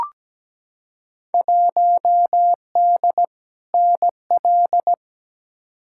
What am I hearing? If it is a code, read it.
Morse code, heard as a keyed tone, about 17 wpm, 700 Hz. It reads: 1D NL
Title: Field Day Exchanges Encore - 17wpm